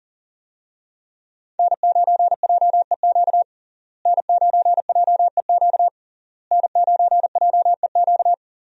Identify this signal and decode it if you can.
Morse code, heard as a keyed tone, about 40 wpm, 700 Hz. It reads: N9JEQ N9JEQ N9JEQ
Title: USA Call Signs - 5 Characters - Repeated 3x - 40wpm